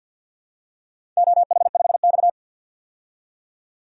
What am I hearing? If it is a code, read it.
Morse code, heard as a keyed tone, about 53 wpm, 700 Hz. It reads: OH5X
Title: DXCC-100 Call Signs - 4 Characters - 50wpm